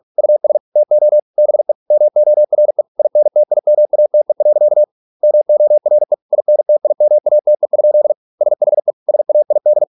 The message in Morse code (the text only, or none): US TO BE MORE INTIMATE. MORE INTIMATE? SHE SAID